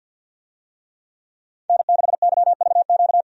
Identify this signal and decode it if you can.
Morse code, heard as a keyed tone, about 46 wpm, 700 Hz. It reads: N6YVQ
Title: USA Call Signs - 5 Characters - 50wpm